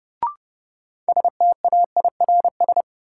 Morse code, heard as a keyed tone, about 30 wpm, 700 Hz. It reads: STAIRS
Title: Sets of 5 Words Encore - New Word Drill - 30wpm